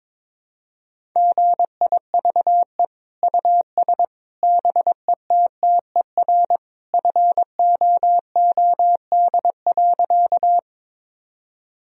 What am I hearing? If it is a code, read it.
Morse code, heard as a keyed tone, about 22 wpm, 700 Hz. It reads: GIVE US BETTER FOOD.
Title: Sentences from Top 1000 Words - Review Part 1 - 22wpm